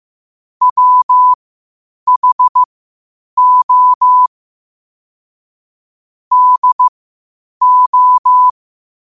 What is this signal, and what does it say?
Morse code, heard as a keyed tone, about 15 wpm, spaced as for 8 wpm, 1000 Hz. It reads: WHO DO
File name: 8wpm